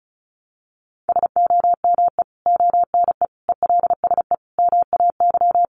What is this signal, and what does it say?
Morse code, heard as a keyed tone, about 34 wpm, 700 Hz. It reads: SOME ONE ELSE MAY
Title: Sentences from Top 1100 Words 35wpm